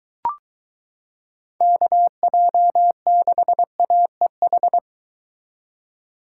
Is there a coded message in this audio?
Yes: Morse code from 1.6 to 4.8 s, heard as a keyed tone, about 23 wpm, 700 Hz. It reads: KJ6AEH